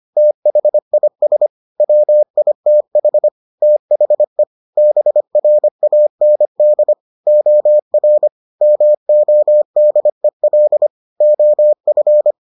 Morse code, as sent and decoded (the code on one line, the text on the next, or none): - .... .. ... / .-- .. - .... / - .... / -... .-. .- -. -.. / --- .-. / -- --- -.. .-.. / --- ..-.
THIS WITH THE BRAND OR MODEL OF